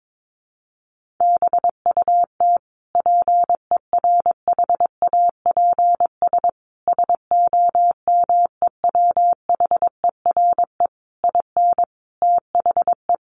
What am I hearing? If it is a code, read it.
Morse code, heard as a keyed tone, about 22 wpm, 700 Hz. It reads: BUT PERHAPS SOMEWHERE IN THE
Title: Sentences from Top 1400 Words 22wpm